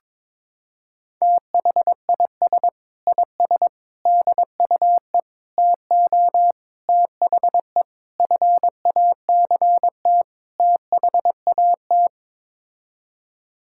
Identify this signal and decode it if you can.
Morse code, heard as a keyed tone, about 21 wpm, 700 Hz. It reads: THIS IS DUE TO THE FACT THAT
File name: practice / 22wpm / Sets of 7 Words - 22wpm